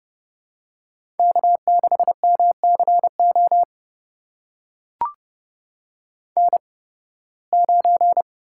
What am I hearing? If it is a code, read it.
Morse code, heard as a keyed tone, about 30 wpm, spaced as for 20 wpm, 700 Hz. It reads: K6MCO N9
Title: USA Call Signs - 5 Characters - 8x Character Spacing - 30wpm